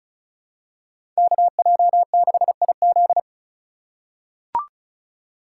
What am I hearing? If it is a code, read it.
Morse code, heard as a keyed tone, about 34 wpm, 700 Hz. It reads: KJ6IZ
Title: USA Call Signs - 5 Characters - 3x Character Spacing - 35wpm